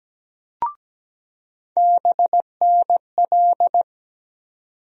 Morse code, heard as a keyed tone, about 17 wpm, 700 Hz. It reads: BNL
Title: USA Call Sign Postfixes - 3 Letters 17wpm